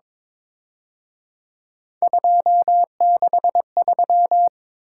Morse code, heard as a keyed tone, about 26 wpm, 700 Hz. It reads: 263
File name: Speed-Racing - Three Letter-Number Combinations - 22wpm